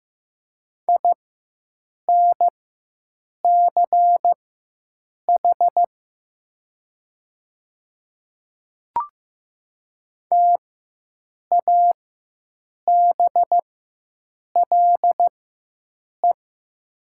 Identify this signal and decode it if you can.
Morse code, heard as a keyed tone, about 15 wpm, spaced as for 7 wpm, 700 Hz. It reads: INCH TABLE
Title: 4x Character Spacing - Top 500 Words - 15wpm